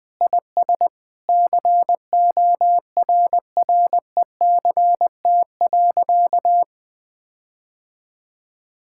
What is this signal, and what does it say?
Morse code, heard as a keyed tone, about 20 wpm, 700 Hz. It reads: IS CORRECT.
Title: Sentences with 8-Word Phrases - 20wpm